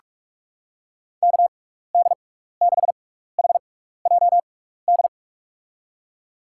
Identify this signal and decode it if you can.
Morse code, heard as a keyed tone, about 46 wpm, 700 Hz. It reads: K D 6 H J D
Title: USA Call Signs - 6 Characters - 6x Character Spacing - 45wpm